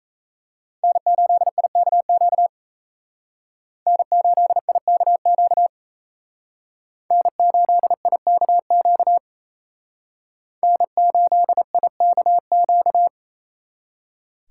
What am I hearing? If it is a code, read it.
Morse code, heard as a keyed tone, about 36 wpm, 700 Hz. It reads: N8IKQ N8IKQ N8IKQ N8IKQ